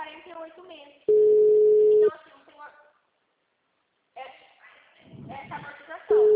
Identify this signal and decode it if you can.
Ringback tone (Europe, 425 Hz), with speech.